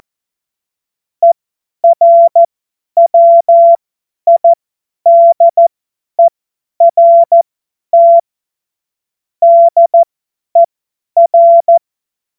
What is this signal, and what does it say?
Morse code, heard as a keyed tone, about 14 wpm, spaced as for 10 wpm, 680 Hz. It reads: ERWIDERT DER